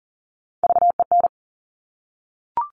Morse code, heard as a keyed tone, about 41 wpm, 700 Hz. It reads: VEN
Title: Top Three Letter Combinations in Words 40wpm